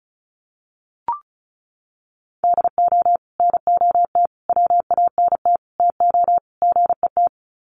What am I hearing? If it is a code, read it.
Morse code, heard as a keyed tone, about 34 wpm, 700 Hz. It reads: DO NOT WANT TO GET
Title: Sets of 5 Words 35wpm